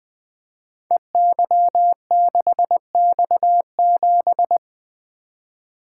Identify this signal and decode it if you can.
Morse code, heard as a keyed tone, about 20 wpm, 700 Hz. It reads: EY6X7
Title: Code Groups - 5 Letter:Numbers - 20wpm